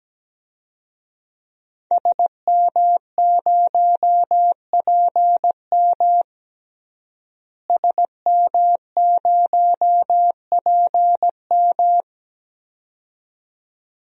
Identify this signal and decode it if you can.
Morse code, heard as a keyed tone, about 17 wpm, 700 Hz. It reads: SM0PM SM0PM
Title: DXCC-100 Call Signs - 5 Characters - Repeated 2x - 17wpm